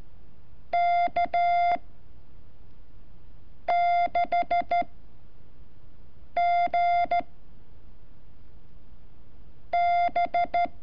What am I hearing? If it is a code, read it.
Morse code, heard as a keyed tone, about 13 wpm, 700 Hz. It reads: K 6 G B
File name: curso / ejercicio 30